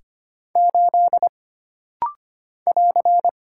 Morse code, heard as a keyed tone, about 25 wpm, 700 Hz. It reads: 8 <AR>
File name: Single Character (Letter:Number:Punctuation:Prosign) - Rapid-Fire - 25wpm